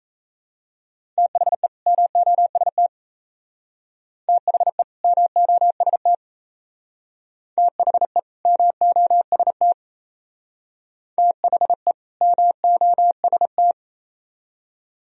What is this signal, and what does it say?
Morse code, heard as a keyed tone, about 35 wpm, 700 Hz. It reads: THE MOST THE MOST THE MOST THE MOST
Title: Speed-Racing - Sets of 2 Words 28wpm